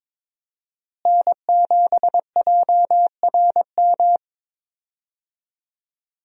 Morse code, heard as a keyed tone, about 22 wpm, 700 Hz. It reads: N7JRM
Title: USA Call Signs - 5 Characters - Repeated 2x - 22wpm